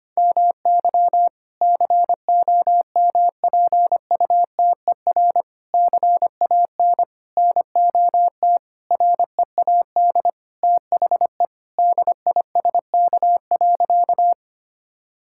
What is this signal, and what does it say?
Morse code, heard as a keyed tone, about 25 wpm, 700 Hz. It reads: MY COMPUTER CAN NOT READ THE DISK.
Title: Sentences from Top 1000 Words - Review Part 6 - 25wpm